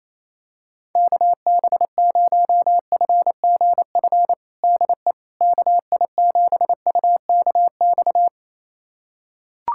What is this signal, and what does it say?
Morse code, heard as a keyed tone, about 28 wpm, 700 Hz. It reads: KB0FGF DE KI7UKX
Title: USA Callsign DE Callsign - 6 Characters - 28wpm